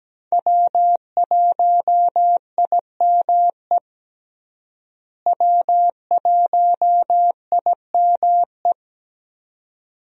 Morse code, heard as a keyed tone, about 17 wpm, 700 Hz. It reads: W1IME W1IME
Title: USA Call Signs - 5 Characters - Repeated 2x - 17wpm